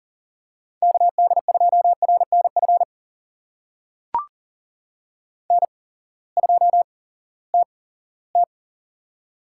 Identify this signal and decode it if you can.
Morse code, heard as a keyed tone, about 40 wpm, spaced as for 20 wpm, 700 Hz. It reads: KD2RNF N2TT
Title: USA Call Signs - 8x Character Spacing - 40wpm